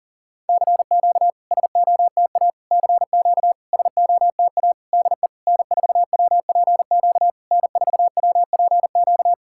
Morse code, heard as a keyed tone, about 40 wpm, 690 Hz. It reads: CQ SOTA CQ SOTA DE N4WPQ N4WPQ
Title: SOTA QSO Practice 40wpm